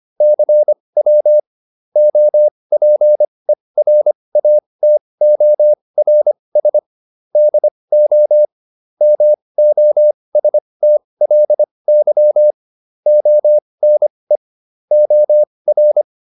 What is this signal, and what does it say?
Morse code, heard as a keyed tone, about 25 wpm, spaced as for 20 wpm, 600 Hz. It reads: CW OPERATORS DO MOSTLY ONE OR